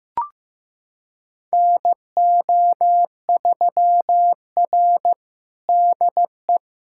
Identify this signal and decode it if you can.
Morse code, heard as a keyed tone, about 15 wpm, 700 Hz. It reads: NO3R DE
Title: USA Callsign DE Callsign - 4 Characters - 15wpm